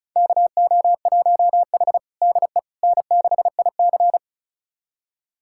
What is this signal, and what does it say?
Morse code, heard as a keyed tone, about 34 wpm, 700 Hz. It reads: KO1H DE N6IC
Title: USA Callsign DE Callsign - 4 Characters - 35wpm